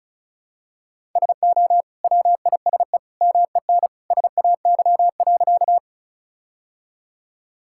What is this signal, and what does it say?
Morse code, heard as a keyed tone, about 36 wpm, 700 Hz. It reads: SO WISE MEN SAY.
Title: Sentences from Top 1600 Words 35wpm